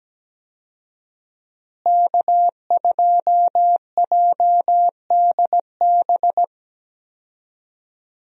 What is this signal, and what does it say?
Morse code, heard as a keyed tone, about 17 wpm, 700 Hz. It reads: K2JDB
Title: USA Call Signs - 5 Characters - Repeated 3x - 17wpm